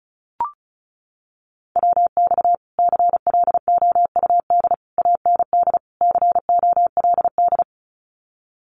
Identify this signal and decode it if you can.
Morse code, heard as a keyed tone, about 34 wpm, 700 Hz. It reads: WX CLOUD AND COLD
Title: Common QSO Phrases Encore 35wpm